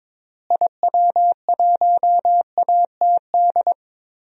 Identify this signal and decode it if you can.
Morse code, heard as a keyed tone, about 21 wpm, 700 Hz. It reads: IW1ATD
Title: DXCC-100 Call Signs - 6 Characters - Repeated 3x - 22wpm